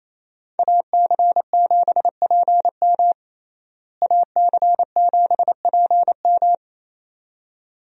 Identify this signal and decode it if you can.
Morse code, heard as a keyed tone, about 28 wpm, 700 Hz. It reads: AC7PM AC7PM
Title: USA Call Signs - 5 Characters - Repeated 2x - 28wpm